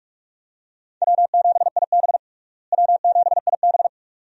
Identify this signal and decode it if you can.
Morse code, heard as a keyed tone, about 48 wpm, 700 Hz. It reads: W7IB W7IB
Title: USA Call Signs - 4 Characters - Repeated 2x - 45wpm